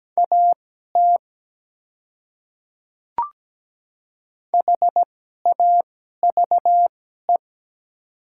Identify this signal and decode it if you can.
Morse code, heard as a keyed tone, about 17 wpm, spaced as for 12 wpm, 700 Hz. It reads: AT HAVE
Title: 2x Character Spacing - Top 100 Words - 17wpm